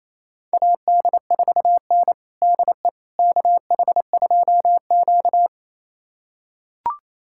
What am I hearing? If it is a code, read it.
Morse code, heard as a keyed tone, about 28 wpm, 700 Hz. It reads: AD4N DE KH2Q
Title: USA Callsign DE Callsign - 4 Characters - 28wpm